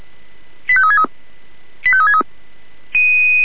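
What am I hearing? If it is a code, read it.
Five-tone selective call (ZVEI), heard as quick runs of five tones, tones of about 70 ms. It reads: 85363 85363
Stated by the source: ZVEI five-tone selective call received on an SDR.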